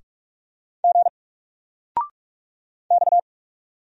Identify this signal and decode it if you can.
Morse code, heard as a keyed tone, about 44 wpm, 700 Hz. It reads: G X
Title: Single Character (Letter:Number:Punctuation:Prosign) 45wpm